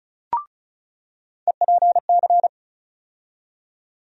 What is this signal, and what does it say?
Morse code, heard as a keyed tone, about 34 wpm, 700 Hz. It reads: EPC
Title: USA Call Sign Postfixes - 3 Letters 35wpm